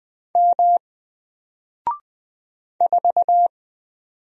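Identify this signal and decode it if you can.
Morse code, heard as a keyed tone, about 20 wpm, 700 Hz. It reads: M 4
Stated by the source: Single letter and number Morse drill at 20 wpm, from N4LCD.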